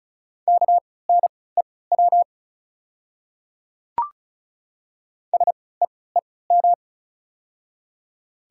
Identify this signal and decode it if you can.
Morse code, heard as a keyed tone, about 36 wpm, spaced as for 20 wpm, 690 Hz. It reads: KNEW SEEM
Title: 3x Character Spacing - Top 500 Words - 35wpm